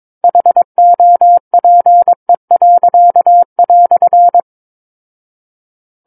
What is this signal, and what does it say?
Morse code, heard as a keyed tone, about 22 wpm, 700 Hz. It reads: HOPE."
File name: Book One / 13-Chapter 13 - 22 WPM